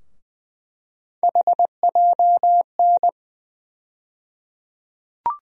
Morse code, heard as a keyed tone, about 21 wpm, 700 Hz. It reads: HJN
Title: USA Call Sign Postfixes - 3 Letters 20wpm